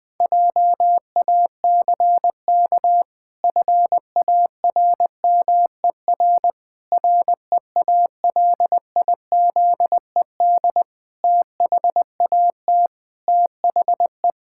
Morse code, heard as a keyed tone, about 20 wpm, 700 Hz. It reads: JACK FARMER REALIZED THAT THE
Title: Pangram Challenge v2 - 20wpm